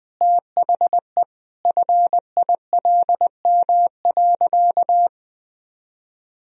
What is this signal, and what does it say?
Morse code, heard as a keyed tone, about 19 wpm, 700 Hz. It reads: THE FILM.
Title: Sentences from Top 1000 Words - Review Part 3 - 20wpm